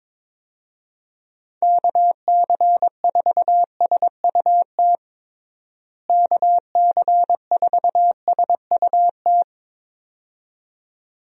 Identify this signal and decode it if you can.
Morse code, heard as a keyed tone, about 22 wpm, 700 Hz. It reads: KC4SUT KC4SUT
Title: USA Call Signs - 6 Characters - Repeated 2x - 22wpm